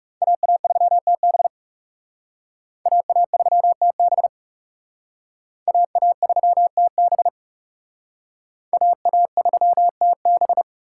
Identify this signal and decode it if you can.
Morse code, heard as a keyed tone, about 37 wpm, 700 Hz. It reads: AA3TB AA3TB AA3TB AA3TB
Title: Speed Racing - USA 2x2 Call Signs 30wpm